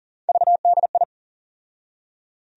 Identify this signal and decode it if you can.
Morse code, heard as a keyed tone, about 40 wpm, 700 Hz. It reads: VDI
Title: USA Call Sign Postfixes - 3 Letters 40wpm